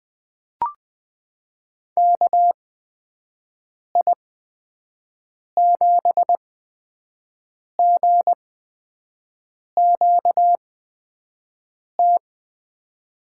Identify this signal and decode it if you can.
Morse code, heard as a keyed tone, about 20 wpm, spaced as for 5 wpm, 700 Hz. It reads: KI7GQT
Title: USA Call Signs - 8x Character Spacing - 20wpm